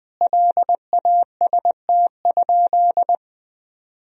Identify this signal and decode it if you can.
Morse code, heard as a keyed tone, about 21 wpm, 700 Hz. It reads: LAST?
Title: Sentences from Top 1000 Words - Review Part 2 - 20wpm